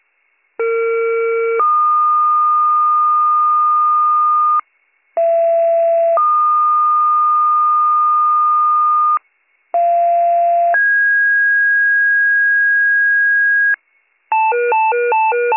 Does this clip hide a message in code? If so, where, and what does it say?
Yes: two-tone page from 0.6 to 13.7 s, heard as a one-second tone, then a three-second tone, about 460 then 1220 Hz, about 670 then 1220 Hz, about 690 then 1650 Hz.